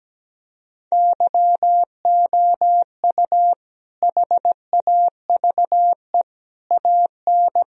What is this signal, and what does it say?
Morse code, heard as a keyed tone, about 17 wpm, 700 Hz. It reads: YOU HAVE AN
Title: Sentences from Top 1000 Words - Review Part 1 - 17wpm